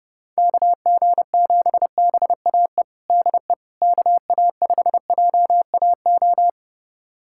Morse code, heard as a keyed tone, about 30 wpm, 700 Hz. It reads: KG7BAE DE KA5JAO
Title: USA Callsign DE Callsign - 6 Characters - 30wpm